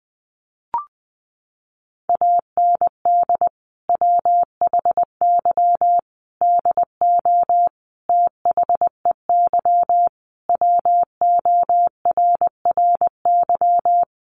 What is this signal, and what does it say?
Morse code, heard as a keyed tone, about 20 wpm, 700 Hz. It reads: AND WHY DO THEY WORRY
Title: Sentences from Top 1100 Words 20wpm